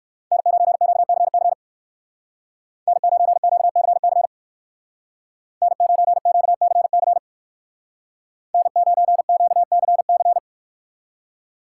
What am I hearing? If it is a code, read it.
Morse code, heard as a keyed tone, about 62 wpm, 700 Hz. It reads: N9QCC N9QCC N9QCC N9QCC